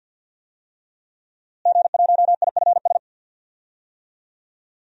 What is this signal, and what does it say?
Morse code, heard as a keyed tone, about 50 wpm, 700 Hz. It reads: G1IFS
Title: DXCC-100 Call Signs - 5 Characters - Repeated 3x - 50wpm